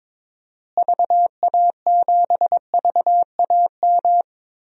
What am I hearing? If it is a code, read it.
Morse code, heard as a keyed tone, about 22 wpm, 700 Hz. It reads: VA7VAM
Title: DXCC-100 Call Signs - 6 Characters - Repeated 2x - 22wpm